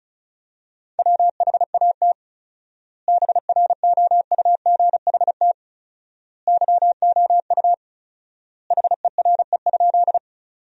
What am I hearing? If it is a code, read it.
Morse code, heard as a keyed tone, about 36 wpm, 700 Hz. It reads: WHAT BROUGHT YOU HERE?
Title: 4x Word-Spacing - Sentences from Top 500 Words - 35wpm